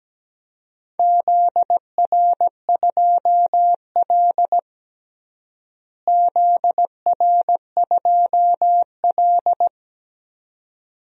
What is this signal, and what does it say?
Morse code, heard as a keyed tone, about 17 wpm, 700 Hz. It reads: ZR2L ZR2L
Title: DXCC-100 Call Signs - 4 Characters - Repeated 2x - 17wpm